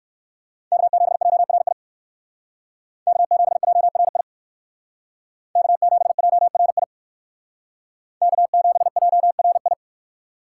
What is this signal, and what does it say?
Morse code, heard as a keyed tone, about 58 wpm, 700 Hz. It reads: K7JRI K7JRI K7JRI K7JRI